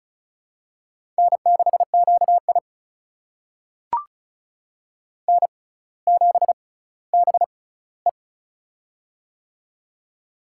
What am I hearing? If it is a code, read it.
Morse code, heard as a keyed tone, about 36 wpm, spaced as for 18 wpm, 700 Hz. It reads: N6QI N7BE
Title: USA Call Signs - 6x Character Spacing - 35wpm